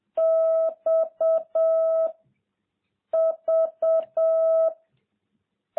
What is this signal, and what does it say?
Morse code, heard as a keyed tone, about 7 wpm, 660 Hz. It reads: X V